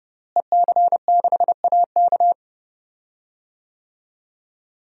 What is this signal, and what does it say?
Morse code, heard as a keyed tone, about 29 wpm, 700 Hz. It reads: EC6AK
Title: DXCC-100 Call Signs - 5 Characters - Repeated 2x - 30wpm